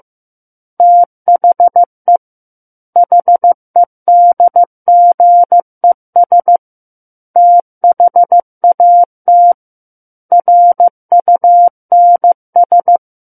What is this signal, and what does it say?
Morse code, heard as a keyed tone, about 15 wpm, 700 Hz. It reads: THE HEDGES THAT RUNS
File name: Book One / 06-Chapter 6 - 15 WPM